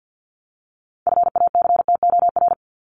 Morse code, heard as a keyed tone, about 50 wpm, 700 Hz. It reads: FACTOR
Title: ES:S Suffix - Top 100 Words with Base Words 50wpm